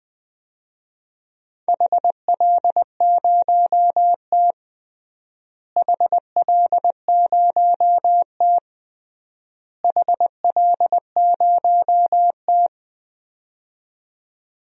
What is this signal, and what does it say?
Morse code, heard as a keyed tone, about 20 wpm, 700 Hz. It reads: HL0T HL0T HL0T
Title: DXCC-100 Call Signs - 4 Characters - Repeated 3x - 20wpm